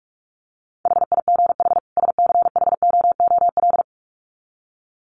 Morse code, heard as a keyed tone, about 44 wpm, 700 Hz. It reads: HIGH SCHOOL
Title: Sets of 2 Words 45wpm